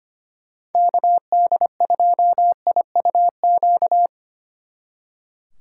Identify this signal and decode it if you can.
Morse code, heard as a keyed tone, about 31 wpm, 700 Hz. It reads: KD2IUQ
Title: Speed Racing - USA 2x3 Call Signs 25wpm